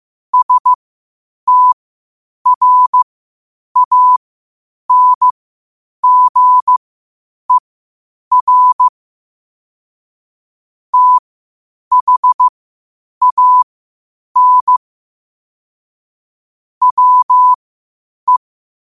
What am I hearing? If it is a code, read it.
Morse code, heard as a keyed tone, about 15 wpm, spaced as for 8 wpm, 1000 Hz. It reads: STRANGER THAN WE
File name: qotdfiles / 8wpm